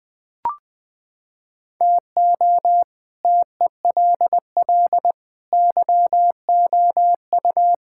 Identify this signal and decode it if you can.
Morse code, heard as a keyed tone, about 20 wpm, 700 Hz. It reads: TO TELL YOU
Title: Sentences with 5-Word Phrases - 20wpm